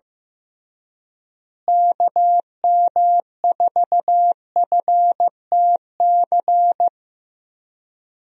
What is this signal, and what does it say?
Morse code, heard as a keyed tone, about 15 wpm, 700 Hz. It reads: KM4FTC
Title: USA Call Signs - 6 Characters - Repeated 3x - 15wpm